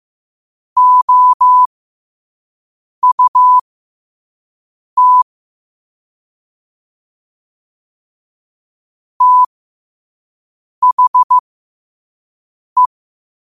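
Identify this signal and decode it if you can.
Morse code, heard as a keyed tone, about 15 wpm, spaced as for 5 wpm, 1000 Hz. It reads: OUT THE
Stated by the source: Morse code quote practice at 5 words per minute.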